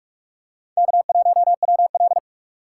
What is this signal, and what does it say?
Morse code, heard as a keyed tone, about 44 wpm, 700 Hz. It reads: K1WL